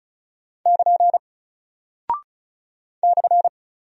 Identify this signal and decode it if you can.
Morse code, heard as a keyed tone, about 36 wpm, 700 Hz. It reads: <KN> /
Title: Single Character (Letter:Number:Punctuation:Prosign) Alt 35wpm